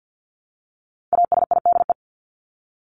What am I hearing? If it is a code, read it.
Morse code, heard as a keyed tone, about 50 wpm, 700 Hz. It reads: ASIDE
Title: Top 1751-2000 Words 50wpm